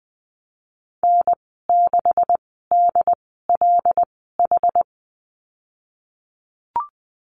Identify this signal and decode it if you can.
Morse code, heard as a keyed tone, about 20 wpm, spaced as for 15 wpm, 700 Hz. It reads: N6DLH
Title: 193 Lesson 31g - Letter v - Call Signs 20wpm